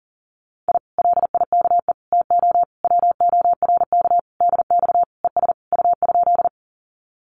Morse code, heard as a keyed tone, about 40 wpm, 700 Hz. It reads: I LIKE TO WORK DX ES U?
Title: Common QSO Phrases Encore 40wpm